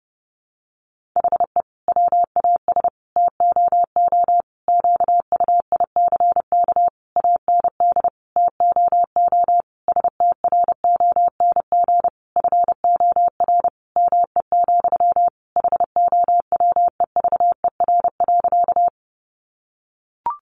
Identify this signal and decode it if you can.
Morse code, heard as a keyed tone, about 30 wpm, 700 Hz. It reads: HE WAS TOO QUICK AND TOO STRONG FOR ME, HOWEVER.
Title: Sentences from Top 600 Words 30wpm